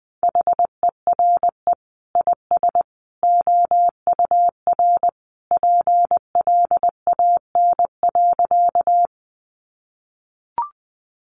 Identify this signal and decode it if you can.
Morse code, heard as a keyed tone, about 20 wpm, 700 Hz. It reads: HERE IS OUR PLAN.
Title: Sentences from Top 500 Words 20wpm